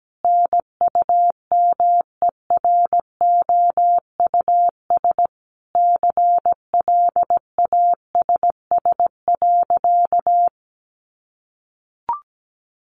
Morse code, heard as a keyed tone, about 17 wpm, 700 Hz. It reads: NUMEROUS CLASS.
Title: Sentences from Top 1400 Words 17wpm